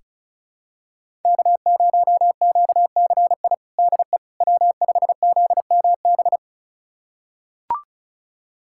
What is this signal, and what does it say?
Morse code, heard as a keyed tone, about 36 wpm, 700 Hz. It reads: K0QCI DE W5ZMB
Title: USA Callsign DE Callsign - 5 Characters - 35wpm